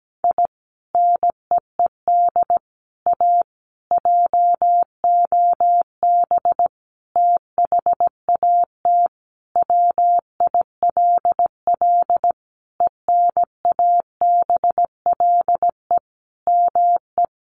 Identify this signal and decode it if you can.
Morse code, heard as a keyed tone, about 17 wpm, 700 Hz. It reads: I NEED A JOB THAT WILL ENABLE ME
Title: Sentences from Top 1500 Words 17wpm